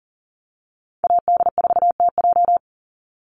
Morse code, heard as a keyed tone, about 40 wpm, 700 Hz. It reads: AD4TJ
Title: Call Signs - 40wpm